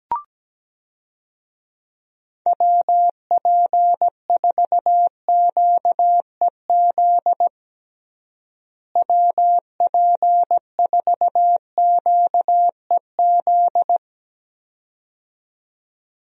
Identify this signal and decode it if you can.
Morse code, heard as a keyed tone, about 17 wpm, 700 Hz. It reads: WP4QEZ WP4QEZ
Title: USA Call Signs - 6 Characters - Repeated 2x - 17wpm